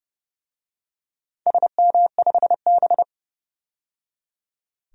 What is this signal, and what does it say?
Morse code, heard as a keyed tone, about 30 wpm, 700 Hz. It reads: SM5B
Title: DXCC-100 Call Signs - 4 Characters - 30wpm